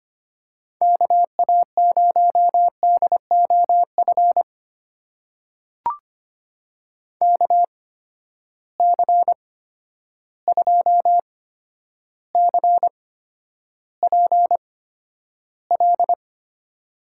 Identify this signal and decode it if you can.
Morse code, heard as a keyed tone, about 25 wpm, spaced as for 11 wpm, 700 Hz. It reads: KA0DOF KC2CPL
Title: USA Call Signs - 6 Characters - 8x Character Spacing - 25wpm